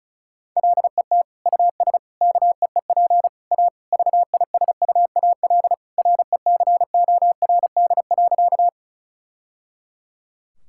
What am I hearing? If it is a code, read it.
Morse code, heard as a keyed tone, about 34 wpm, 700 Hz. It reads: LET US KEEP A VISUAL RECORD.
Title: Sentences from Top 1000 Words - Review Part 6 - 35wpm